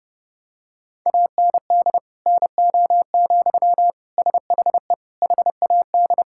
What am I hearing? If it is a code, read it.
Morse code, heard as a keyed tone, about 30 wpm, 700 Hz. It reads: AND NO, SHE HAD
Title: Sentences from Top 1600 Words 30wpm